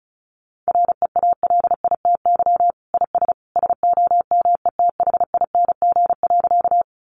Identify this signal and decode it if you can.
Morse code, heard as a keyed tone, about 34 wpm, 700 Hz. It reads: REALITY IS SOMETHING.
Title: Sentences from Top 1200 Words 35wpm